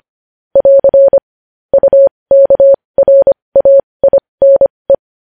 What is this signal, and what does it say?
Morse code, heard as a keyed tone, about 25 wpm, spaced as for 20 wpm, 550 Hz. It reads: <AR> UKRAINE
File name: podcast / News headlines-1x-25x20